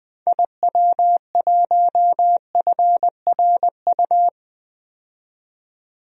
Morse code, heard as a keyed tone, about 20 wpm, 700 Hz. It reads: IW1FRU
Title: DXCC-100 Call Signs - 6 Characters - 20wpm